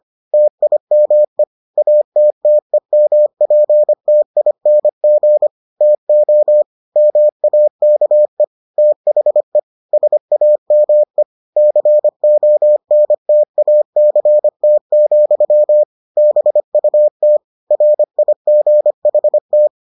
Morse code, heard as a keyed tone, about 25 wpm, 600 Hz. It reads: TIME ATTEMPTING TO MAKE THE SAME CONTACT, BUT RIGHT